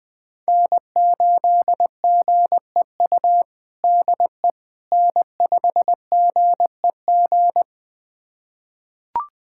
Morse code, heard as a keyed tone, about 20 wpm, 700 Hz. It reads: N8GEU DE N5GEG
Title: USA Callsign DE Callsign - 5 Characters - 20wpm